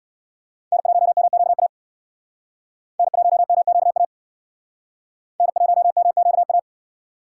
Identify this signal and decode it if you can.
Morse code, heard as a keyed tone, about 61 wpm, 700 Hz. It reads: N1MQA N1MQA N1MQA
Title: Speed Racing - USA 1x3 Call Signs 50wpm